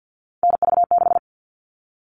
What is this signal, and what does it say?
Morse code, heard as a keyed tone, about 50 wpm, 700 Hz. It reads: NV6
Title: All Cut Numbers - 3-Digit 50wpm